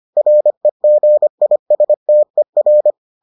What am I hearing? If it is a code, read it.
Morse code, heard as a keyed tone, about 25 wpm, 600 Hz. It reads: REGISTER